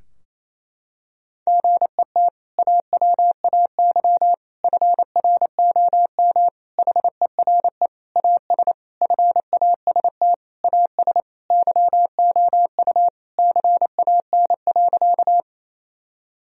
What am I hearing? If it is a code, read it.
Morse code, heard as a keyed tone, about 27 wpm, 700 Hz. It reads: GET AWAY FROM HERE AS FAST AS YOU CAN.
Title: Sentences from Top 1000 Words - Review Part 2 - 28wpm